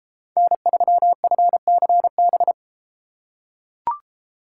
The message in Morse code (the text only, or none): N3FCB